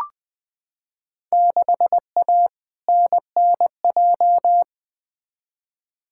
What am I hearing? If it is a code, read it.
Morse code, heard as a keyed tone, about 20 wpm, 700 Hz. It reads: 6A NNJ
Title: Field Day Exchanges Encore - 20wpm